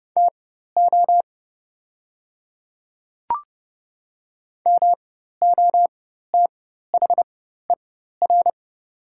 Morse code, heard as a keyed tone, about 30 wpm, spaced as for 14 wpm, 700 Hz. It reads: TO MOTHER
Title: 4x Character Spacing - Top 300 Words - 30wpm